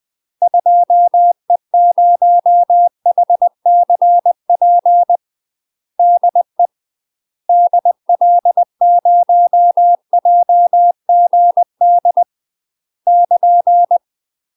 Morse code, heard as a keyed tone, about 20 wpm, 700 Hz. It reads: 2E0HCP DE DL0JGD <KN>
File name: UK MASTER QSOs TRAINING / 20WPM / UK MASTER QSOs 0051